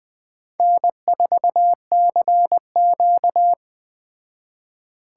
Morse code, heard as a keyed tone, about 20 wpm, 700 Hz. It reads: N4CQ
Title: Speed Racing - USA Call Signs 17wpm